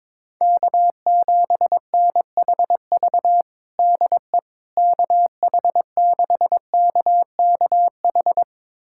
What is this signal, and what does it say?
Morse code, heard as a keyed tone, about 22 wpm, 700 Hz. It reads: K7NHV DE KH6KKH
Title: USA Callsign DE Callsign 22wpm